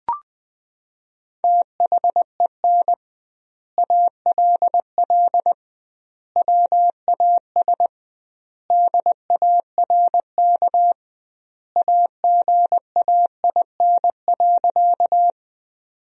Morse code, heard as a keyed tone, about 20 wpm, 700 Hz. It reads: THEN ALL WAS DARK AGAIN.
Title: 2x Word-Spacing - Sentences from Top 500 Words - 20wpm